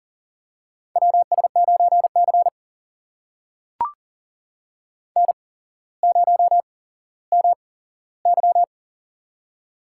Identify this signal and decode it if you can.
Morse code, heard as a keyed tone, about 40 wpm, spaced as for 17 wpm, 700 Hz. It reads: WS9C N0MY
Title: USA Call Signs - 4 Characters - 8x Character Spacing - 40wpm